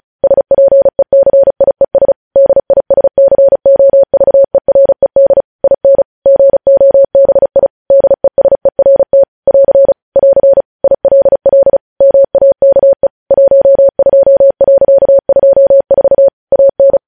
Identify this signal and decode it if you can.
Morse code, heard as a keyed tone, about 36 wpm, 550 Hz. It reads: SPECIES DISCOVERED IN GOBI DESERT <AR> <AR> ILL MAKE 12.24 AN
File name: podcast / News headlines-1x-35x35